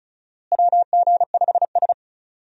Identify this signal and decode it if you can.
Morse code, heard as a keyed tone, about 34 wpm, 700 Hz. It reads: WG5S
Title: USA Call Signs - 4 Characters - Repeated 2x - 35wpm